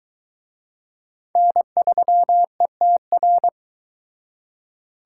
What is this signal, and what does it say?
Morse code, heard as a keyed tone, about 22 wpm, 700 Hz. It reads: N3ETR